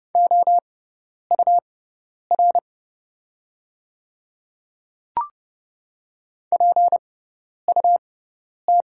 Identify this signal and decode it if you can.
Morse code, heard as a keyed tone, about 30 wpm, spaced as for 10 wpm, 700 Hz. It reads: OUR PUT
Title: 6x Character Spacing - Top 500 Words - 30wpm